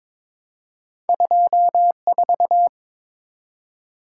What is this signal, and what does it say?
Morse code, heard as a keyed tone, about 23 wpm, 700 Hz. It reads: 24
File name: practice / 22wpm / Two Letter-Number Combinations - 22wpm